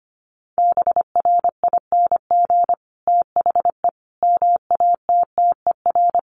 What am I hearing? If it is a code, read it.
Morse code, heard as a keyed tone, about 25 wpm, 700 Hz. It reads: BRING THE MATTER
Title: Sentences from Top 1300 Words 25wpm